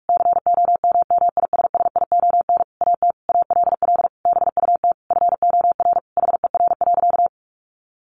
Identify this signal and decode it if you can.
Morse code, heard as a keyed tone, about 44 wpm, 700 Hz. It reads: COMMISSION AT ALL BUT FOR HER.
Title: Sentences from Top 1400 Words 45wpm